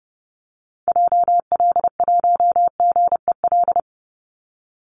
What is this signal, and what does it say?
Morse code, heard as a keyed tone, about 30 wpm, 700 Hz. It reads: JL1GEL
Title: Call Signs - 30wpm